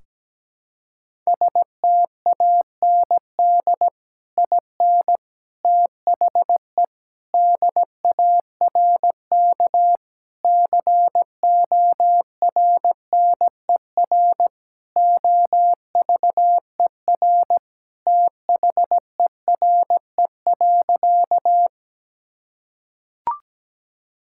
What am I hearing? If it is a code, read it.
Morse code, heard as a keyed tone, about 17 wpm, 700 Hz. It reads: STAND IN THE DARK CORNER OVER THERE.
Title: Sentences from Top 1600 Words 17wpm